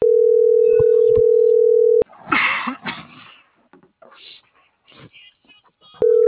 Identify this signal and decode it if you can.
Ringback tone (North America), 440 + 480 Hz.